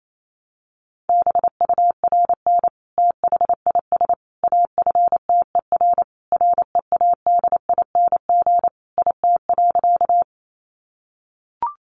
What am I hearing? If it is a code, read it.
Morse code, heard as a keyed tone, about 28 wpm, 700 Hz. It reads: BURN THIS AFTER READING IT.
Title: Sentences from Top 1300 Words 28wpm